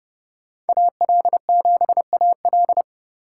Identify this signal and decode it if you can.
Morse code, heard as a keyed tone, about 34 wpm, 700 Hz. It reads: AL7AL
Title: Speed Racing - USA 2x2 Call Signs 30wpm